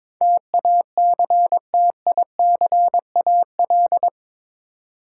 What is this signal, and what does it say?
Morse code, heard as a keyed tone, about 22 wpm, 700 Hz. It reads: TACTICAL
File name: practice / 22wpm / Eight Letter Words - 22wpm